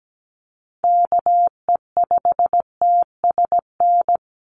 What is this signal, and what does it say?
Morse code, heard as a keyed tone, about 17 wpm, 700 Hz. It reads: KE5TSN
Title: USA Call Signs 17wpm